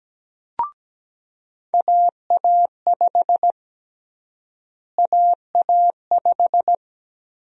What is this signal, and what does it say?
Morse code, heard as a keyed tone, about 17 wpm, 700 Hz. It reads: AA5 AA5
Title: USA Call Sign Prefixes - Repeated 2x - 17wpm